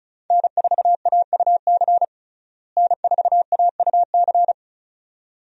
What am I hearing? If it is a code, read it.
Morse code, heard as a keyed tone, about 36 wpm, 700 Hz. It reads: N4AUC N4AUC
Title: USA Call Signs - 5 Characters - Repeated 2x - 35wpm